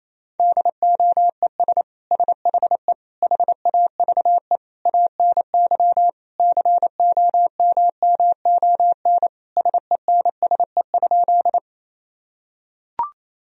Morse code, heard as a keyed tone, about 28 wpm, 700 Hz. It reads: DOES SHE HAVE ANY COMMON SENSE?
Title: Sentences from Top 1000 Words - Review Part 4 - 28wpm